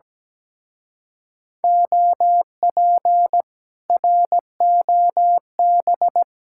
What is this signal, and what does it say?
Morse code, heard as a keyed tone, about 17 wpm, 700 Hz. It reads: OP ROB
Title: Speed-Racing Common QSO Phrases - 17wpm